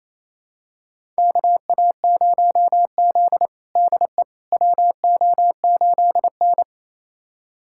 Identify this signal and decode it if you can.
Morse code, heard as a keyed tone, about 28 wpm, 700 Hz. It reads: KA0Z DE WO8N
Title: USA Callsign DE Callsign - 4 Characters - 28wpm